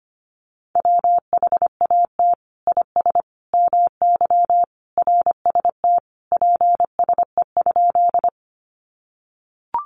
Morse code, heard as a keyed tone, about 25 wpm, 700 Hz. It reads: WHAT IS MY RST PSE?
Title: Common QSO Phrases 25wpm